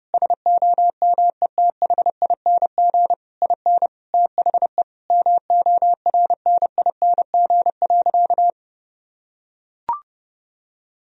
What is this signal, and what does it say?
Morse code, heard as a keyed tone, about 32 wpm, 700 Hz. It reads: SOMETHING IN THE MORNING.
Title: Sentences from Top 1000 Words - Review Part 4 - 30wpm